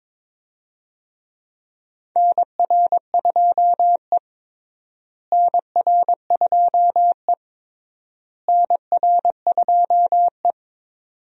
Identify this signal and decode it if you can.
Morse code, heard as a keyed tone, about 22 wpm, 700 Hz. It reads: NR2E NR2E NR2E
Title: USA Call Signs - 4 Characters - Repeated 3x - 22wpm